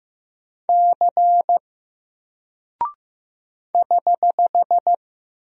Morse code, heard as a keyed tone, about 15 wpm, 700 Hz. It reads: C <HH>
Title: Single Character (Letter:Number:Punctuation:Prosign) Alt 15wpm